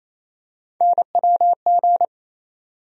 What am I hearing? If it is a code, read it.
Morse code, heard as a keyed tone, about 28 wpm, 700 Hz. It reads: NWG
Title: USA Call Sign Postfixes - 3 Letters 28wpm